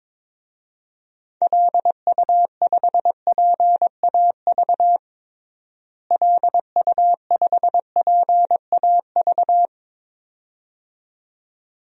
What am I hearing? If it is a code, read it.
Morse code, heard as a keyed tone, about 22 wpm, 700 Hz. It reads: LU5PAV LU5PAV
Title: DXCC-100 Call Signs - 6 Characters - Repeated 2x - 22wpm